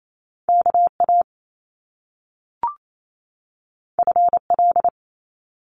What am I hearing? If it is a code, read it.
Morse code, heard as a keyed tone, about 28 wpm, 700 Hz. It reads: KA FL
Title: Top Two Letter Combinations in Words 28wpm